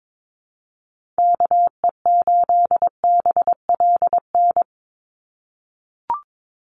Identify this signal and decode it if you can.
Morse code, heard as a keyed tone, about 22 wpm, 700 Hz. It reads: KE8BLN
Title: USA Call Signs 22wpm